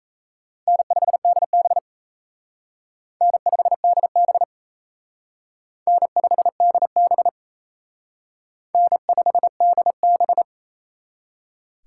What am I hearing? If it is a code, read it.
Morse code, heard as a keyed tone, about 34 wpm, 700 Hz. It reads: N5DB N5DB N5DB N5DB
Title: Speed Racing - USA 1x2 Call Signs 28wpm